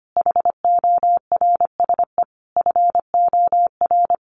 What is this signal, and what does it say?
Morse code, heard as a keyed tone, about 25 wpm, 700 Hz. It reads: HORSE FOR
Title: Sentences from Top 1400 Words 25wpm